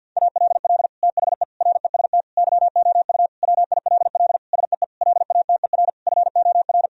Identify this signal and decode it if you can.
Morse code, heard as a keyed tone, about 50 wpm, 700 Hz. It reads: ALL THE REST YOU WILL SEE LATER FOR
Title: Sentences from Top 1000 Words - Review Part 4 - 50wpm